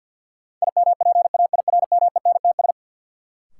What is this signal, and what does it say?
Morse code, heard as a keyed tone, about 47 wpm, 700 Hz. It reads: IMPAIRMENTS
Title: Sets of 5 Words Encore - New Word Drill - 50wpm